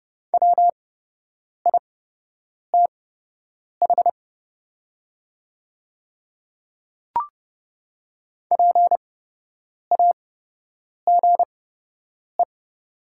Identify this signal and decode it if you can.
Morse code, heard as a keyed tone, about 30 wpm, spaced as for 8 wpm, 700 Hz. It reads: WITH PAGE
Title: 8x Character Spacing - Top 500 Words - 30wpm